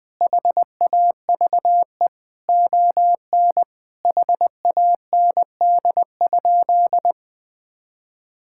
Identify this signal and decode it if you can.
Morse code, heard as a keyed tone, about 20 wpm, 700 Hz. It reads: HAVE ON HAND?
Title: Sentences from Top 1000 Words - Review Part 5 - 20wpm